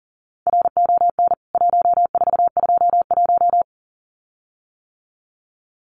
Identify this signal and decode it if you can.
Morse code, heard as a keyed tone, about 40 wpm, 700 Hz. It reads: RON 1421
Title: CWT Exchange 40wpm